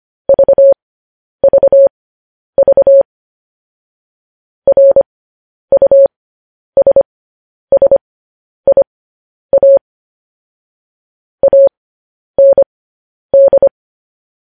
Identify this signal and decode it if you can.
Morse code, heard as a keyed tone, about 25 wpm, spaced as for 10 wpm, 550 Hz. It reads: VVV RUSSIA AND
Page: news-headlines-in-morse-code-at-10-wpm